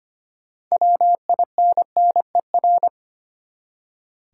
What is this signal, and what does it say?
Morse code, heard as a keyed tone, about 25 wpm, 700 Hz. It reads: WINNER
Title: Six Letter Words - 25wpm